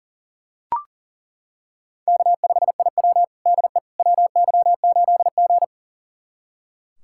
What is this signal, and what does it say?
Morse code, heard as a keyed tone, about 37 wpm, 700 Hz. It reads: K5IW DE WY8G
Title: USA Callsign DE Callsign - 4 Characters - 40wpm